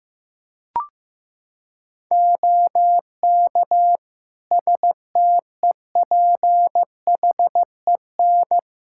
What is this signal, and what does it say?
Morse code, heard as a keyed tone, about 15 wpm, 700 Hz. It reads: OK STEPHEN
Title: Practice QSO - Exchange 2 - Info - 15wpm